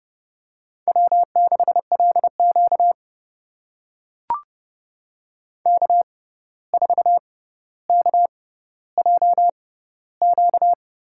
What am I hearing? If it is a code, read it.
Morse code, heard as a keyed tone, about 30 wpm, spaced as for 14 wpm, 700 Hz. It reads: W6LQ K4KJQ
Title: USA Call Signs - 6x Character Spacing - 30wpm